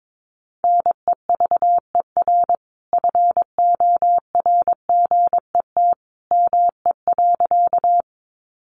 Morse code, heard as a keyed tone, about 22 wpm, 700 Hz. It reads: NEVER FORGET ME.
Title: Sentences from Top 1200 Words 22wpm